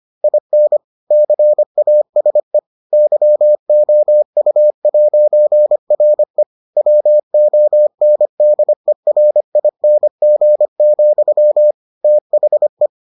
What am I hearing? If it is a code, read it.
Morse code, heard as a keyed tone, about 25 wpm, 600 Hz. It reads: IN CASE YOU'RE WONDERING, THE